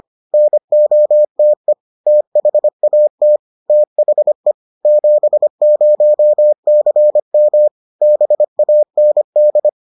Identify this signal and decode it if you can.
Morse code, heard as a keyed tone, about 25 wpm, 600 Hz. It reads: NOTE THAT THE 70CM BAND